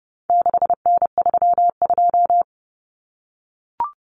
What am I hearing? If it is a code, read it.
Morse code, heard as a keyed tone, about 30 wpm, 700 Hz. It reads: 6N32
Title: Common Cut Numbers - 4-Digit 30wpm